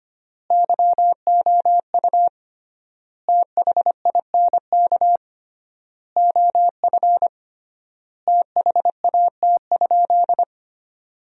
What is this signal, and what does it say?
Morse code, heard as a keyed tone, about 25 wpm, 700 Hz. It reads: YOU THINK OF THAT?
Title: 3x Word-Spacing - Sentences from Top 200 Words - 25wpm